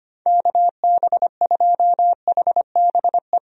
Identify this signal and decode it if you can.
Morse code, heard as a keyed tone, about 25 wpm, 700 Hz. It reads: KB2HBE
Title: USA Call Signs - 6 Characters - 25wpm